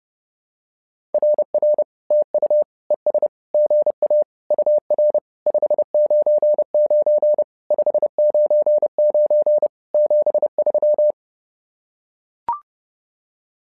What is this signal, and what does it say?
Morse code, heard as a keyed tone, about 29 wpm, 600 Hz. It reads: RR TU ES GA UR 599 599 73
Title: SOTA QSO Practice 30wpm